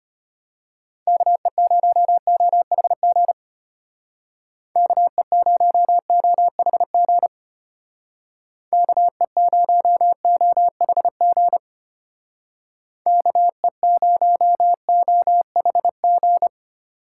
Morse code, heard as a keyed tone, about 31 wpm, 700 Hz. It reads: KE0OHG KE0OHG KE0OHG KE0OHG
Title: Speed Racing - USA 2x3 Call Signs 25wpm